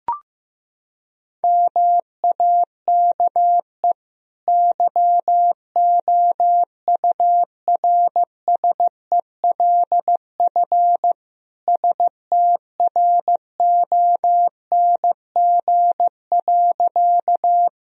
Morse code, heard as a keyed tone, about 15 wpm, 700 Hz. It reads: MAKE YOURSELF STRONG.
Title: Sentences from Top 1000 Words - Review Part 3 - 15wpm